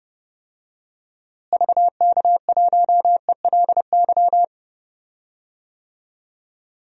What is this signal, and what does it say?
Morse code, heard as a keyed tone, about 30 wpm, 700 Hz. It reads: VK1ELY
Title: DXCC-100 Call Signs - 6 Characters - Repeated 3x - 30wpm